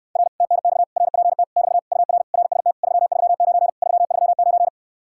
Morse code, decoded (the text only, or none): R TNX RPT <BT> UR RST 339 339